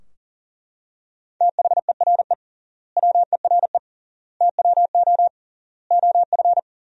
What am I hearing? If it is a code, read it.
Morse code, heard as a keyed tone, about 40 wpm, 700 Hz. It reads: THERE WERE TWO OF
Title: 3x Word-Spacing - Sentences from Top 100 Words - 40wpm